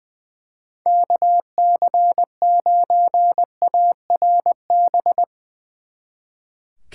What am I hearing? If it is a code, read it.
Morse code, heard as a keyed tone, about 23 wpm, 700 Hz. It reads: KC9ARB
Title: Speed Racing - USA Call Signs 20wpm